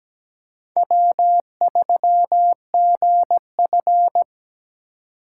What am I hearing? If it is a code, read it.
Morse code, heard as a keyed tone, about 17 wpm, 700 Hz. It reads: W3GF
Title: USA Call Signs - 4 Characters - 3x Character Spacing - 17wpm